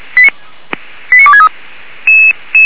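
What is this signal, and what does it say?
Five-tone selective call (ZVEI), heard as quick runs of five tones, tones of about 70 ms. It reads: 89263
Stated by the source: ZVEI five-tone selective call 89277, received on SDR by Feuerwehr Orlamünde.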